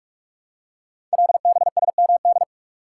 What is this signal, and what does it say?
Morse code, heard as a keyed tone, about 44 wpm, 700 Hz. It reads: LBSMD
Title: Code Groups - 5 Letters - 45wpm